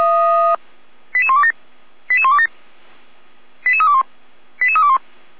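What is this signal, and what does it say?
Five-tone selective call (ZVEI), heard as quick runs of five tones, tones of about 70 ms. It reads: 88127 88127 88321 88321
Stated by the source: ZVEI five-tone call 88235 received by SDR, fire brigade (Feuerwehr) Orlamünde.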